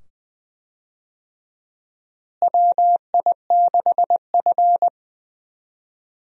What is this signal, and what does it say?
Morse code, heard as a keyed tone, about 21 wpm, 700 Hz. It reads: WI6F
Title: USA Call Signs - 4 Characters - Repeated 2x - 20wpm